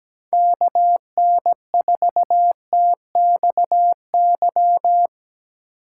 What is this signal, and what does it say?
Morse code, heard as a keyed tone, about 18 wpm, 700 Hz. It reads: KN4TXY
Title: Speed Racing - USA 2x3 Call Signs 17wpm